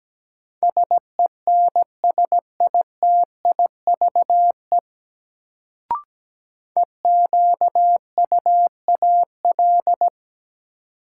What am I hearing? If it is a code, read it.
Morse code, heard as a keyed tone, about 17 wpm, 700 Hz. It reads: SENSITIVE EQUAL
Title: Sentences from Top 1400 Words - New Word Drill Alt - 17wpm